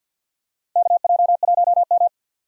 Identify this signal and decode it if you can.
Morse code, heard as a keyed tone, about 47 wpm, 700 Hz. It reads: KJ1M
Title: USA Call Signs - 4 Characters - 50wpm